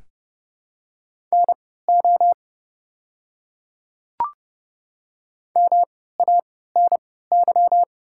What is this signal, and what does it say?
Morse code, heard as a keyed tone, about 30 wpm, spaced as for 17 wpm, 700 Hz. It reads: NO MANY
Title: 3x Character Spacing - Top 100 Words - 30wpm